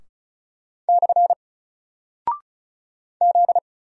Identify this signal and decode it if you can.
Morse code, heard as a keyed tone, about 36 wpm, 700 Hz. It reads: / Z
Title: Single Character (Letter:Number:Punctuation:Prosign) 35wpm